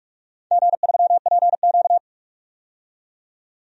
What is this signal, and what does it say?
Morse code, heard as a keyed tone, about 46 wpm, 700 Hz. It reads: G3PQ
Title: DXCC-100 Call Signs - 4 Characters - 45wpm